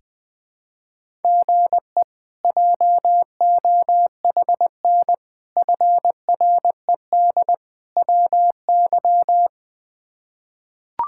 Morse code, heard as a keyed tone, about 20 wpm, 700 Hz. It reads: GE JOHN FRED WY
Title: K1USN SST QSO Practice - 20wpm